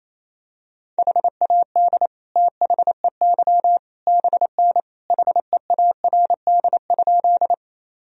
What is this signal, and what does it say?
Morse code, heard as a keyed tone, about 28 wpm, 700 Hz. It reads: HAD THEY BN HEARD?
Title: Sentences from Top 400 Words and Abbreviations 28wpm